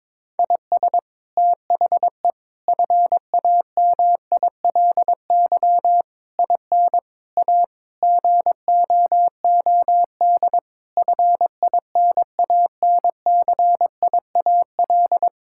Morse code, heard as a keyed tone, about 22 wpm, 700 Hz. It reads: IS THE FAMILY IN A GOOD FINANCIAL
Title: Sentences from Top 1000 Words - Review Part 5 - 22wpm